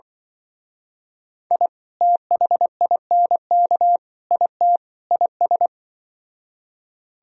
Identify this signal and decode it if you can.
Morse code, heard as a keyed tone, about 24 wpm, 700 Hz. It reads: I THINK IT IS
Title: Speed-Racing - Sets of 4 Words 20wpm